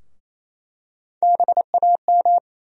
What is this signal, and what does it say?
Morse code, heard as a keyed tone, about 27 wpm, 700 Hz. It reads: BAM
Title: USA Call Sign Postfixes - 3 Letters 28wpm